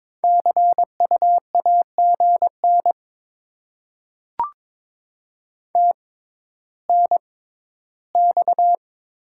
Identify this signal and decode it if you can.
Morse code, heard as a keyed tone, about 22 wpm, spaced as for 12 wpm, 700 Hz. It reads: CU AGN TNX
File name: Top 20 QSO Elements - 6x Character Spacing - 22wpm